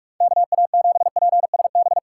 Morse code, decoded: KA7PSB

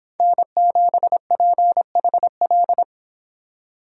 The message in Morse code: N7PHL

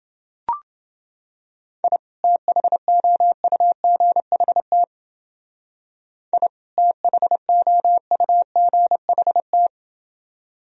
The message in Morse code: I THOUGHT I THOUGHT

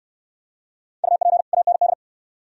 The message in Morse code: UP AND

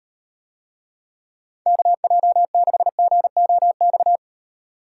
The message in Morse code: KJ6GOX

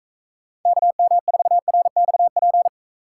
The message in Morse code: KM4RXP

